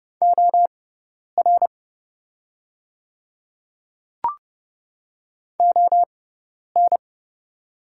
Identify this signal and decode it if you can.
Morse code, heard as a keyed tone, about 30 wpm, spaced as for 10 wpm, 700 Hz. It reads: OR ON
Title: 6x Character Spacing - Top 100 Words - 30wpm